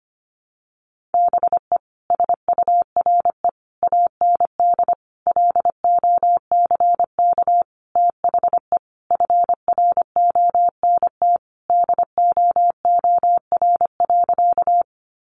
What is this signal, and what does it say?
Morse code, heard as a keyed tone, about 25 wpm, 700 Hz. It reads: BE SURE AND LOCK THE FRONT DOOR.
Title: Sentences from Top 1300 Words 25wpm